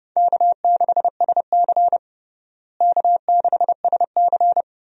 Morse code, heard as a keyed tone, about 32 wpm, 700 Hz. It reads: K6SC K6SC
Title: USA Call Signs - 4 Characters - Repeated 2x - 30wpm